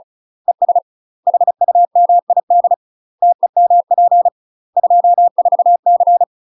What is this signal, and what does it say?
Morse code, heard as a keyed tone, about 35 wpm, 700 Hz. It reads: ES HUMID TEMP 24C
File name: UK MASTER QSOs TRAINING / 35WPM / UK MASTER QSOs 0037